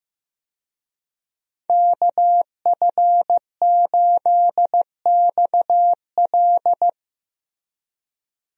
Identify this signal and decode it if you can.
Morse code, heard as a keyed tone, about 15 wpm, 700 Hz. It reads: KF8XL
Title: USA Call Signs - 5 Characters - Repeated 2x - 15wpm